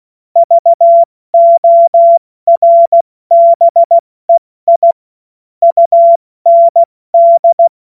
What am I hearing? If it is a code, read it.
Morse code, heard as a keyed tone, about 16 wpm, 680 Hz. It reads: VORBEI UND
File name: Morsekurs / Witze / Witz 07.November 2023 70 F80